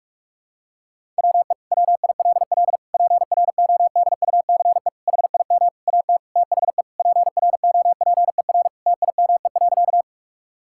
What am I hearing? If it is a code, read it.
Morse code, heard as a keyed tone, about 45 wpm, 700 Hz. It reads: WE WILL PRODUCE HIM AT THE PROPER TIME.
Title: Sentences from Top 1000 Words - Review Part 5 - 45wpm